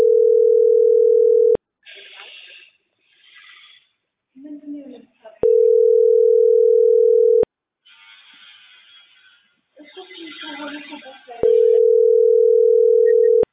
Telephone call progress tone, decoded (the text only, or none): ringback tone